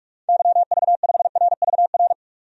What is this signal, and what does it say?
Morse code, heard as a keyed tone, about 41 wpm, 700 Hz. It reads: YV5RVR
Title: DXCC-100 Call Signs - 6 Characters - 45wpm